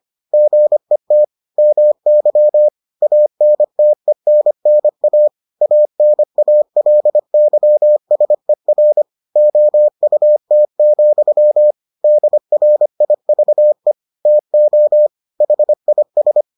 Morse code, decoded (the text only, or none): GET MY ANTENNA ANALYSER OUT, DRIVE TO HIS